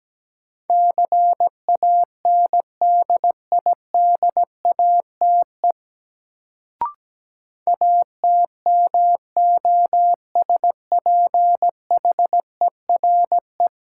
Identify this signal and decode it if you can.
Morse code, heard as a keyed tone, about 17 wpm, 700 Hz. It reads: CANDIDATE ATMOSPHERE
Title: Sentences from Top 1600 Words - New Word Drill - 17wpm